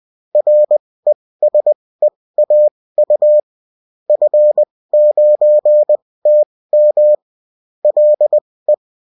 Morse code, heard as a keyed tone, about 20 wpm, spaced as for 16 wpm, 600 Hz. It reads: RESEAU F9TM LE
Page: bulletin-uft-sous-forme-texte-et-en-cw